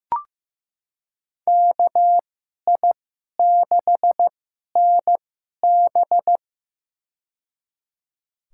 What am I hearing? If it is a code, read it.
Morse code, heard as a keyed tone, about 15 wpm, spaced as for 11 wpm, 700 Hz. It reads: KI6NB
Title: USA Call Signs - 5 Characters - 2x Character Spacing - 15wpm